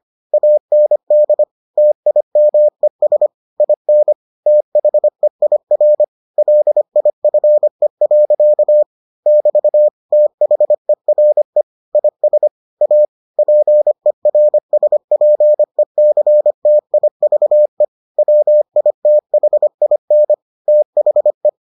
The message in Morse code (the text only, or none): AND TIMES IN THEIR LIFE. <BT> THERE IS A PERSPECTIVE WITHIN THE